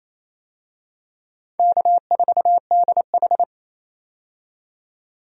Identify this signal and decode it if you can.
Morse code, heard as a keyed tone, about 28 wpm, 700 Hz. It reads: K4DH
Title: USA Call Signs - 4 Characters - 28wpm